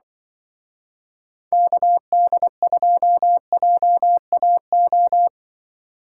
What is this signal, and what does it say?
Morse code, heard as a keyed tone, about 24 wpm, 700 Hz. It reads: KD2JAO
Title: Speed Racing - USA 2x3 Call Signs 20wpm